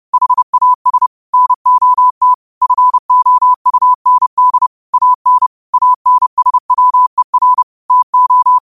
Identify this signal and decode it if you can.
Morse code, heard as a keyed tone, about 30 wpm, 1000 Hz. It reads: HAS NOT FOUND AN ANSWER TO
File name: 30wpm